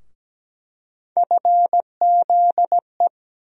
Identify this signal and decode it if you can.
Morse code, heard as a keyed tone, about 17 wpm, 700 Hz. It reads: FZE
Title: USA Call Sign Postfixes - 3 Letters 17wpm